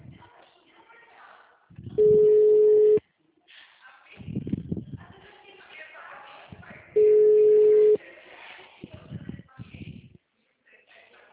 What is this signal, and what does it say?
Ringback tone (Europe, 425 Hz), with speech.